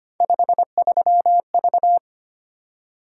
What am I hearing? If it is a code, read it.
Morse code, heard as a keyed tone, about 25 wpm, 700 Hz. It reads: 53V
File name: Three Letter-Number Combinations - 25wpm